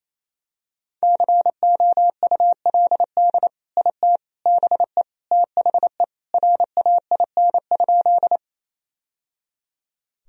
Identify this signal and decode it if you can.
Morse code, heard as a keyed tone, about 27 wpm, 700 Hz. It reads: COULD IT BE THE RAIN?
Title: Sentences from Top 1000 Words - Review Part 1 - 28wpm